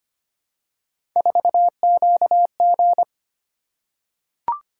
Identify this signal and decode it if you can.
Morse code, heard as a keyed tone, about 25 wpm, 700 Hz. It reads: 4QG
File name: Three Letter-Number Combinations - 25wpm